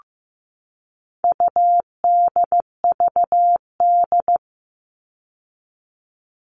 Morse code, heard as a keyed tone, about 15 wpm, 700 Hz. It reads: UDVD
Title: All Cut Numbers - 4-Digit 15wpm